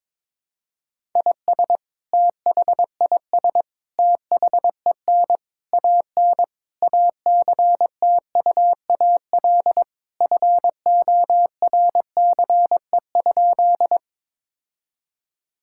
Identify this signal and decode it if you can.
Morse code, heard as a keyed tone, about 22 wpm, 700 Hz. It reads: IS THIS THEN AN ACTUAL FORCE?
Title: Sentences from Top 1000 Words - Review Part 5 - 22wpm